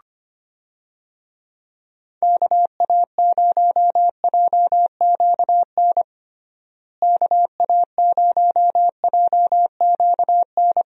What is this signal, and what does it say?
Morse code, heard as a keyed tone, about 25 wpm, 700 Hz. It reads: KA0JQN KA0JQN
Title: USA Call Signs - 6 Characters - Repeated 3x - 25wpm